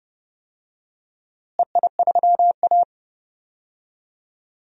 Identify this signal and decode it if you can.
Morse code, heard as a keyed tone, about 31 wpm, 700 Hz. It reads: EI3A
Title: DXCC-100 Call Signs - 4 Characters - Repeated 2x - 30wpm